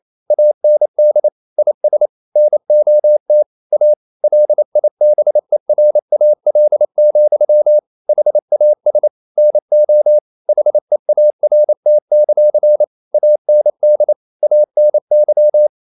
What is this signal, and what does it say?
Morse code, heard as a keyed tone, about 28 wpm, 600 Hz. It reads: AND IS NOT A LIBERAL, HAS NO HEART; AND ANY
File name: CW / Practice / EnglishQuotations0228